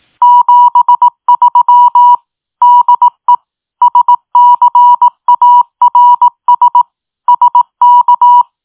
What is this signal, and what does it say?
Morse code, heard as a keyed tone, about 18 wpm, 970 Hz. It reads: 73 DE SCARS SK
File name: tin-upgrade / QSO 2024 01 18 .mp3